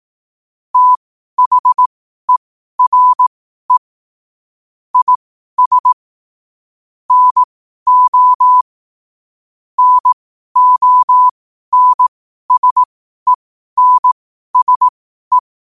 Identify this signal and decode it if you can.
Morse code, heard as a keyed tone, about 18 wpm, spaced as for 12 wpm, 1000 Hz. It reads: THERE IS NO NONSENSE
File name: qotdfiles / 12wpm